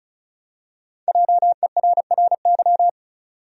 Morse code, heard as a keyed tone, about 34 wpm, 700 Hz. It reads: JERRY
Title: USA Common Names - Max 5 Letters - 35wpm